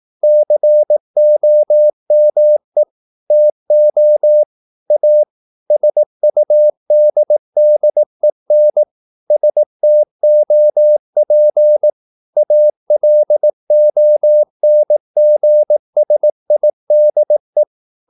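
Morse code, heard as a keyed tone, about 18 wpm, 600 Hz. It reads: COME TO A SUDDEN STOP ALONGSIDE
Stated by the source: Morse practice at 18 wpm, English text, from CW med Gnister.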